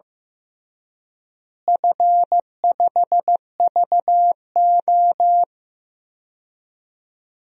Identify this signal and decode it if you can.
Morse code, heard as a keyed tone, about 15 wpm, 700 Hz. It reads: F5VO
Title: DXCC-100 Call Signs - 4 Characters - Repeated 3x - 15wpm